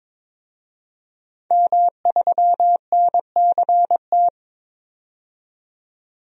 Morse code, heard as a keyed tone, about 23 wpm, 700 Hz. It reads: M3NCT
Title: DXCC-100 Call Signs - 5 Characters - 22wpm